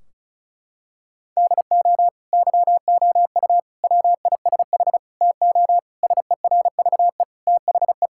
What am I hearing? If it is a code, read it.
Morse code, heard as a keyed tone, about 36 wpm, 700 Hz. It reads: DO YOU WISH TO SERVE THE
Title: Sentences from Top 1600 Words 35wpm